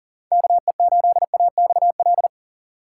Morse code, heard as a keyed tone, about 40 wpm, 700 Hz. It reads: KE8AXL
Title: USA Call Signs - 6 Characters - 2x Character Spacing - 40wpm